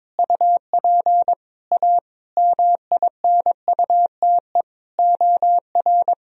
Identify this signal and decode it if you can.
Morse code, heard as a keyed tone, about 22 wpm, 700 Hz. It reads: UP A MINUTE OR
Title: Sentences from Top 1000 Words - Review Part 1 - 22wpm